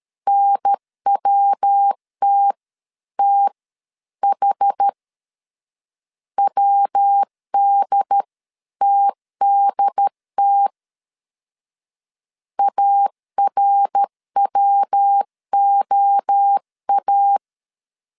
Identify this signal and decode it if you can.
Morse code, heard as a keyed tone, about 13 wpm, spaced as for 11 wpm, 800 Hz. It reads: NWTTH WDTDT ARWOA
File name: CW / DET / 1 3 1500C S A 12wpm 2905